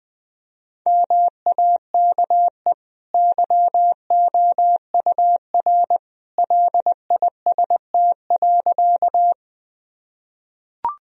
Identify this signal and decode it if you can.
Morse code, heard as a keyed tone, about 20 wpm, 700 Hz. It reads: MAKE YOUR LIST.
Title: Sentences from Top 1000 Words - Review Part 2 - 20wpm